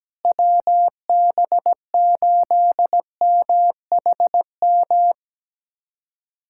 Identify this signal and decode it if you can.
Morse code, heard as a keyed tone, about 17 wpm, 700 Hz. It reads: WB8MHM
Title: USA Callsign DE Callsign - 6 Characters - 17wpm